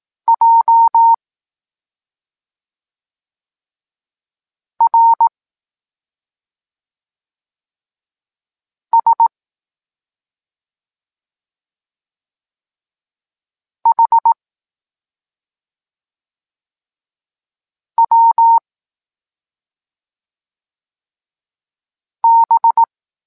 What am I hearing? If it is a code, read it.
Morse code, heard as a keyed tone, about 18 wpm, 920 Hz. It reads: J R S H W B
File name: Tekst-23 4